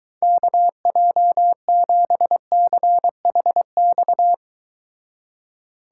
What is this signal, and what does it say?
Morse code, heard as a keyed tone, about 22 wpm, 700 Hz. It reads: KJ7CHX